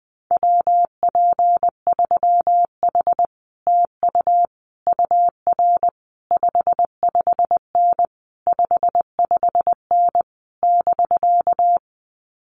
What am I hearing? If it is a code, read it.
Morse code, heard as a keyed tone, about 20 wpm, 700 Hz. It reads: WP3H TU UR 55N 55N <BK>
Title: Common QSO Phrases Encore 20wpm